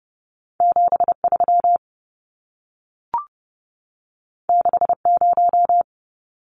Morse code, heard as a keyed tone, about 31 wpm, 700 Hz. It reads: 73 60
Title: Numbers - 2-Digit 30wpm